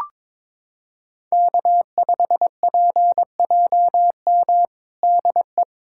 Morse code, heard as a keyed tone, about 23 wpm, 700 Hz. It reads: K5PJM DE
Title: USA Callsign DE Callsign - 5 Characters - 22wpm